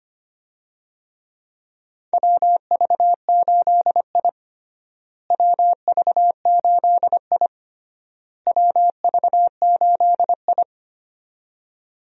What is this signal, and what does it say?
Morse code, heard as a keyed tone, about 25 wpm, 700 Hz. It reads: WV8I WV8I WV8I
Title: USA Call Signs - 4 Characters - Repeated 3x - 25wpm